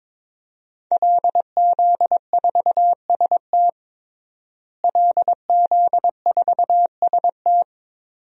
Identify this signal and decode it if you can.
Morse code, heard as a keyed tone, about 22 wpm, 700 Hz. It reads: LZ4ST LZ4ST
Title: DXCC-100 Call Signs - 5 Characters - Repeated 2x - 22wpm